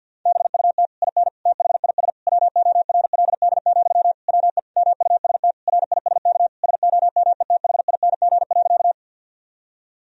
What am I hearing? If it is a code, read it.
Morse code, heard as a keyed tone, about 50 wpm, 700 Hz. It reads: BUT IN THIS WORLD, WE MUST RISK SOMETHING.
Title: Sentences from Top 1000 Words - Review Part 4 - 50wpm